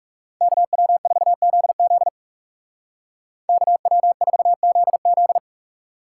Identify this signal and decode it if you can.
Morse code, heard as a keyed tone, about 40 wpm, 700 Hz. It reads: KW4ZZ KW4ZZ